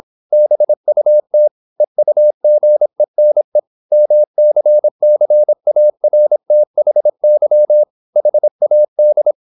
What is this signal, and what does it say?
Morse code, heard as a keyed tone, about 26 wpm, 600 Hz. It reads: BUT EUGENE MCCARTHY HAD
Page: Never 26wpm | CW med Gnister